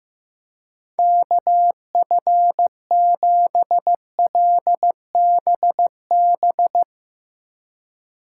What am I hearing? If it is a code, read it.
Morse code, heard as a keyed tone, about 15 wpm, 700 Hz. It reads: KF7LBB
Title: Speed Racing - USA 2x3 Call Signs 15wpm